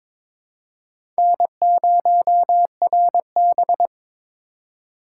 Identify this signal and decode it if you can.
Morse code, heard as a keyed tone, about 22 wpm, 700 Hz. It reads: N0RB
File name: practice / 22wpm / Four Letter-Number Combinations - 22wpm